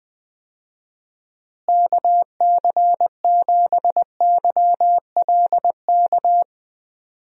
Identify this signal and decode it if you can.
Morse code, heard as a keyed tone, about 20 wpm, 700 Hz. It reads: KC7YLK
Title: USA Call Signs - 6 Characters - Repeated 2x - 20wpm